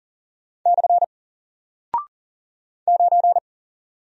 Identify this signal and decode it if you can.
Morse code, heard as a keyed tone, about 40 wpm, 700 Hz. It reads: / 9
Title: Single Character (Letter:Number:Punctuation:Prosign) Alt 40wpm